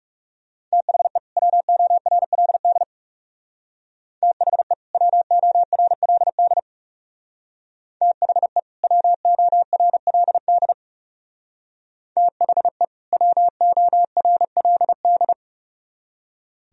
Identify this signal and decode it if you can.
Morse code, heard as a keyed tone, about 36 wpm, 700 Hz. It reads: THE WORLD THE WORLD THE WORLD THE WORLD
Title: Speed-Racing - Sets of 2 Words 30wpm